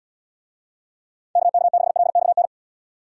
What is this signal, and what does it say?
Morse code, heard as a keyed tone, about 77 wpm, 700 Hz. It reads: KK6LXN